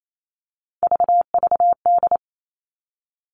Morse code, heard as a keyed tone, about 28 wpm, 700 Hz. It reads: VVD
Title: Three Challenging Letter-Number Combinations 28wpm